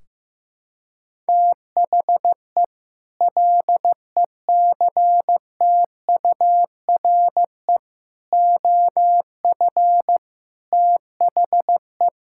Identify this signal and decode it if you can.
Morse code, heard as a keyed tone, about 15 wpm, 700 Hz. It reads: THE LECTURE OF THE
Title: Sentences from Top 1600 Words 15wpm